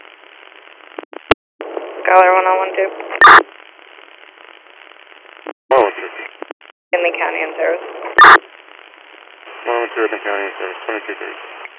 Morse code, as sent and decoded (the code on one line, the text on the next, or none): none